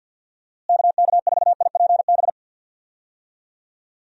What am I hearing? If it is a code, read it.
Morse code, heard as a keyed tone, about 53 wpm, 700 Hz. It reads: KK4IPB
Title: USA Call Signs - 6 Characters - Repeated 2x - 50wpm